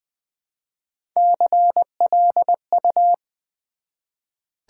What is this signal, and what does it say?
Morse code, heard as a keyed tone, about 20 wpm, 700 Hz. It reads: CLU
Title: Build A Word - 1 Letter At A Time - 4-Letter Words - 20wpm